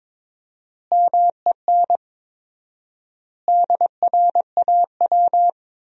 Morse code, heard as a keyed tone, about 22 wpm, 700 Hz. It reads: MEN DRAW
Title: 4x Word-Spacing - Sentences from Top 300 Words - 22wpm